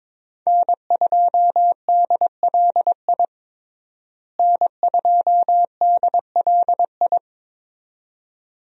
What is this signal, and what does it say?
Morse code, heard as a keyed tone, about 22 wpm, 700 Hz. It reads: N2DLI N2DLI
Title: USA Call Signs - 5 Characters - Repeated 2x - 22wpm